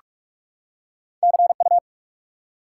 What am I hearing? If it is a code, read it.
Morse code, heard as a keyed tone, about 46 wpm, 700 Hz. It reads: CU
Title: Top 90 QSO Elements 45wpm